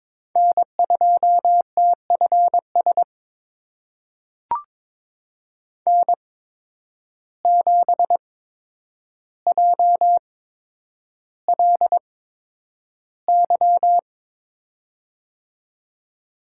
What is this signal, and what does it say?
Morse code, heard as a keyed tone, about 22 wpm, spaced as for 9 wpm, 700 Hz. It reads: N2TFS N7JLY
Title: USA Call Signs - 5 Characters - 8x Character Spacing - 22wpm